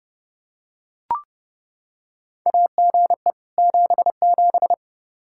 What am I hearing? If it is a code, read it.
Morse code, heard as a keyed tone, about 30 wpm, 700 Hz. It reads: AGE 77
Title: Practice QSO - Exchange 3 - More Info - 30wpm